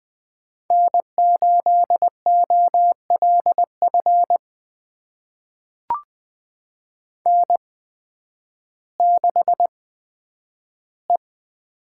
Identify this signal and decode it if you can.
Morse code, heard as a keyed tone, about 20 wpm, spaced as for 11 wpm, 700 Hz. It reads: N8OLF N6E
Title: USA Call Signs - 8x Character Spacing - 20wpm